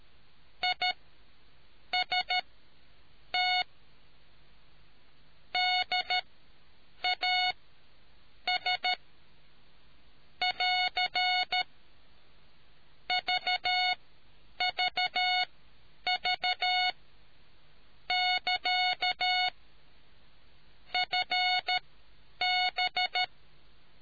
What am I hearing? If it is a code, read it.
Morse code, heard as a keyed tone, about 13 wpm, spaced as for 7 wpm, 730 Hz. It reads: IST DAS <AR> VVV<KA> FB